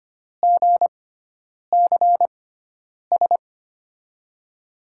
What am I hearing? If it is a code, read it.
Morse code, heard as a keyed tone, about 25 wpm, spaced as for 9 wpm, 700 Hz. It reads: GCS
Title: USA Call Signs - 5 Characters - 6x Character Spacing - 25wpm